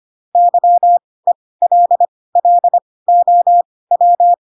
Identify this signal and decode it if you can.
Morse code, heard as a keyed tone, about 25 wpm, spaced as for 18 wpm, 700 Hz. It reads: YELLOW
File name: Pinocchio / Pinocchio ch2 12 25 ik1hge